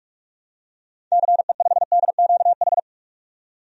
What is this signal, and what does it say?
Morse code, heard as a keyed tone, about 46 wpm, 700 Hz. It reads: CE5DQH
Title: DXCC-100 Call Signs - 6 Characters - 45wpm